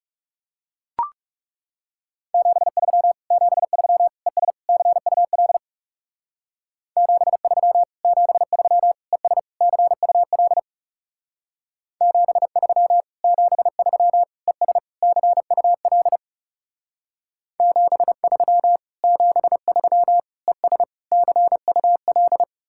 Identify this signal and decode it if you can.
Morse code, heard as a keyed tone, about 36 wpm, 700 Hz. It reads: 73 73 ES CUL 73 73 ES CUL 73 73 ES CUL 73 73 ES CUL
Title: Speed-Racing Common QSO Phrases - 30wpm